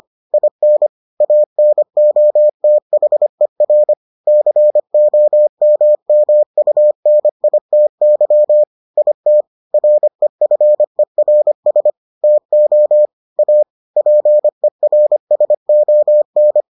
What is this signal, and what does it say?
Morse code, heard as a keyed tone, about 25 wpm, 600 Hz. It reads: IN ANOTHER COMMUNITY IT REFERS TO A PERSON